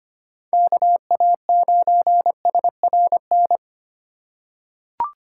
Morse code, heard as a keyed tone, about 25 wpm, 700 Hz. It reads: KA9SRN
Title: USA Call Signs - 6 Characters - 3x Character Spacing - 25wpm